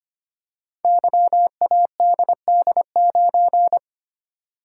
Morse code, heard as a keyed tone, about 25 wpm, 700 Hz. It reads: YADD9
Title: Code Groups - 5 Letter:Numbers - 25wpm